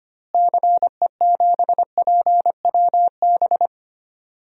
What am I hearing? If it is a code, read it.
Morse code, heard as a keyed tone, about 25 wpm, 700 Hz. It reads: CE7PWB
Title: DXCC-100 Call Signs - 6 Characters - Repeated 2x - 25wpm